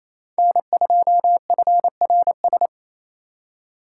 Morse code, heard as a keyed tone, about 36 wpm, 700 Hz. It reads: N2FRS